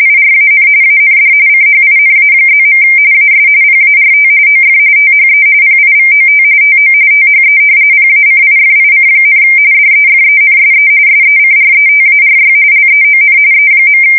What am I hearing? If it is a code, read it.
RTTY, about 45.45 baud. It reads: IRAN AND CHINA+. AS PART OF THE PIVOT TO THE PACIFIC, DEFENSE SECRETARY LEON E.
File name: fsk170 / usnavy5